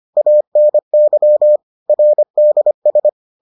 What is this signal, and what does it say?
Morse code, heard as a keyed tone, about 25 wpm, 600 Hz. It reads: ANY RDS